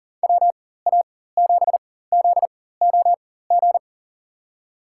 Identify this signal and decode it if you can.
Morse code, heard as a keyed tone, about 40 wpm, spaced as for 19 wpm, 700 Hz. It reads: WA7ZOG
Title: USA Call Signs - 6 Characters - 4x Character Spacing - 40wpm